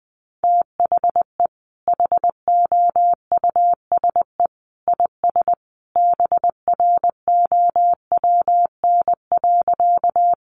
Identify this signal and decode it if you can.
Morse code, heard as a keyed tone, about 20 wpm, 700 Hz. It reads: THE HOUSE IS BROWN.
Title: Sentences from Top 1200 Words 20wpm